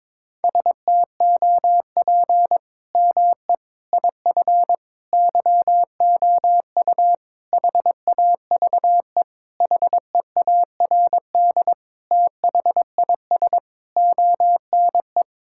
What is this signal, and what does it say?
Morse code, heard as a keyed tone, about 22 wpm, 700 Hz. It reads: STOP ME IF YOU HAVE HEARD THIS ONE
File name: practice / 22wpm / Sentences with 8-Word Phrases - 22wpm